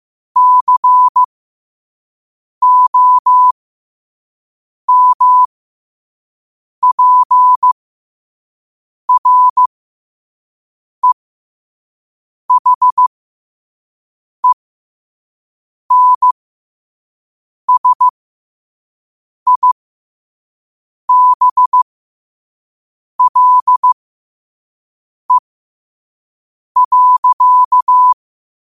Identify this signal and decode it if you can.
Morse code, heard as a keyed tone, about 15 wpm, spaced as for 5 wpm, 1000 Hz. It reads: COMPREHENSIBLE.
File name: 5wpm